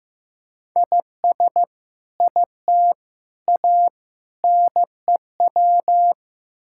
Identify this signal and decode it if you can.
Morse code, heard as a keyed tone, about 15 wpm, 700 Hz. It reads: IS IT A NEW
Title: Sentences from Top 1000 Words - Review Part 5 - 15wpm